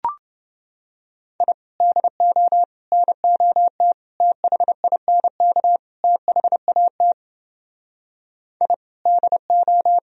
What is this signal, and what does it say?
Morse code, heard as a keyed tone, about 31 wpm, 700 Hz. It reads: I DO NOT THINK THAT I DO